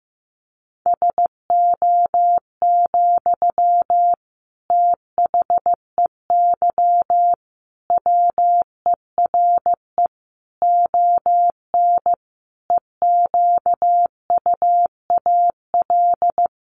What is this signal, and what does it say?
Morse code, heard as a keyed tone, about 15 wpm, 700 Hz. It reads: SO, THEY WERE ON EQUAL
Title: Sentences from Top 1400 Words 15wpm